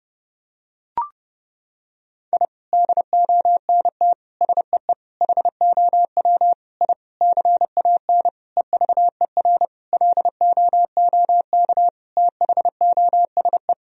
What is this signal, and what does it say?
Morse code, heard as a keyed tone, about 30 wpm, 700 Hz. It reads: I DONT SEE HOW I CAN EVER LOOK THOSE
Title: Sentences from Top 1000 Words - Review Part 2 - 30wpm